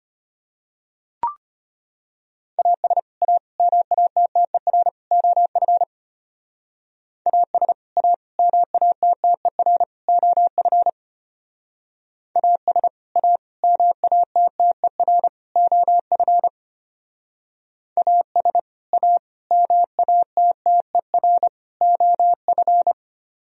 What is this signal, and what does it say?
Morse code, heard as a keyed tone, about 29 wpm, 690 Hz. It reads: AS A MATTER OF AS A MATTER OF AS A MATTER OF AS A MATTER OF
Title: Speed-Racing - Sets of 4 Words 25wpm